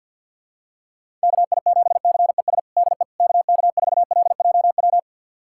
Morse code, heard as a keyed tone, about 46 wpm, 700 Hz. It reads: KI7CES DE KK4LJW
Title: USA Callsign DE Callsign 50wpm